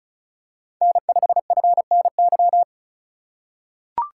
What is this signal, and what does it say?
Morse code, heard as a keyed tone, about 34 wpm, 700 Hz. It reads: N5FNY
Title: USA Call Signs - 3x Character Spacing - 35wpm